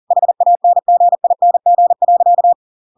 Morse code, heard as a keyed tone, about 40 wpm, 700 Hz. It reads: HANGING.